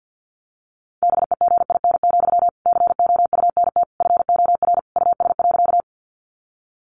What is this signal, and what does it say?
Morse code, heard as a keyed tone, about 50 wpm, 700 Hz. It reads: BEGIN, COUNT FOR US.
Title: Sentences from Top 1500 Words 50wpm